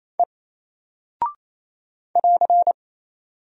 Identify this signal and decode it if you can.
Morse code, heard as a keyed tone, about 28 wpm, 700 Hz. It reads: E <AR>
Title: Single Character (Letter:Number:Punctuation:Prosign) Alt 28wpm